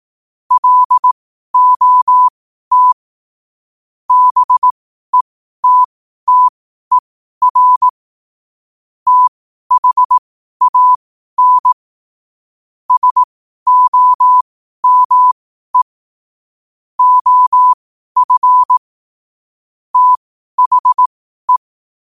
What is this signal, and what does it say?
Morse code, heard as a keyed tone, about 18 wpm, spaced as for 12 wpm, 1000 Hz. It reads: LOT BETTER THAN SOME OF THE
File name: qotdfiles / 12wpm